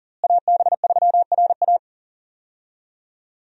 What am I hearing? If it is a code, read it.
Morse code, heard as a keyed tone, about 38 wpm, 700 Hz. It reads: AB3RA
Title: USA Call Signs - 5 Characters - Repeated 3x - 40wpm